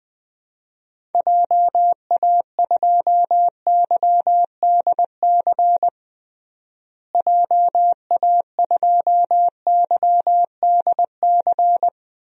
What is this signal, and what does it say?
Morse code, heard as a keyed tone, about 20 wpm, 700 Hz. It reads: JA2YDC JA2YDC
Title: DXCC-100 Call Signs - 6 Characters - Repeated 3x - 20wpm